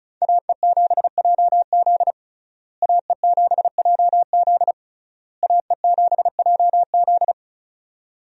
Morse code, heard as a keyed tone, about 36 wpm, 700 Hz. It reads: AE7JZ AE7JZ AE7JZ
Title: USA Call Signs - 5 Characters - Repeated 3x - 35wpm